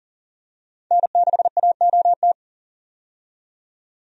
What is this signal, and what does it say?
Morse code, heard as a keyed tone, about 39 wpm, 700 Hz. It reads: N6AOT
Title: USA Call Signs - 5 Characters - 40wpm